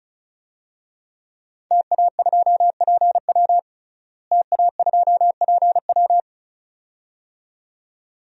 Morse code, heard as a keyed tone, about 34 wpm, 700 Hz. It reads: TA2PW TA2PW
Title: DXCC-100 Call Signs - 5 Characters - Repeated 2x - 35wpm